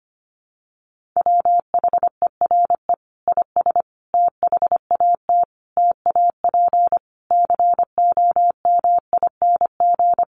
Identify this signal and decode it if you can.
Morse code, heard as a keyed tone, about 25 wpm, 700 Hz. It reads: WHERE IS THAT TAP COMING
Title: Sentences from Top 1400 Words 25wpm